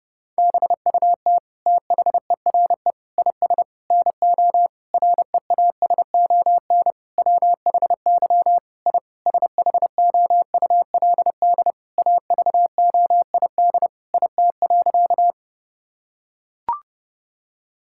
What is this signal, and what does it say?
Morse code, heard as a keyed tone, about 30 wpm, 700 Hz. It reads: BUT THERE IS NO REASON WHY I SHOULD AVOID IT.
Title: Sentences from Top 1000 Words - Review Part 6 - 30wpm